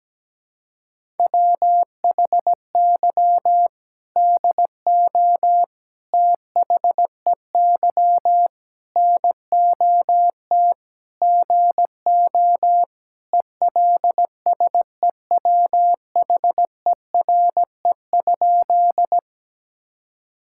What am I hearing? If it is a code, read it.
Morse code, heard as a keyed tone, about 17 wpm, 700 Hz. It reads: WHY DO THEY NOT GO ELSEWHERE?
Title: Sentences from Top 1600 Words 17wpm